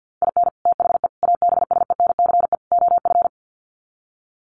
Morse code, heard as a keyed tone, about 48 wpm, 700 Hz. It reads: IN THE ABSENCE OF
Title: Sets of 4 Words 50wpm